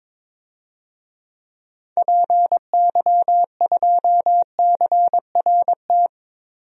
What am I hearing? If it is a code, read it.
Morse code, heard as a keyed tone, about 23 wpm, 700 Hz. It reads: PY2CRT
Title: DXCC-100 Call Signs - 6 Characters - 22wpm